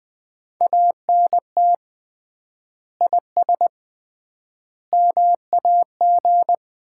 Morse code, heard as a keyed tone, about 20 wpm, 700 Hz. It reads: ANT IS MAG
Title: Common QSO Phrases - 3x Word-Spacing - 20wpm